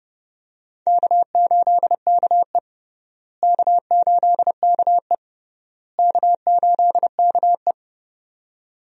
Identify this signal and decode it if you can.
Morse code, heard as a keyed tone, about 30 wpm, 700 Hz. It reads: K8KE K8KE K8KE
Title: USA Call Signs - 4 Characters - Repeated 3x - 30wpm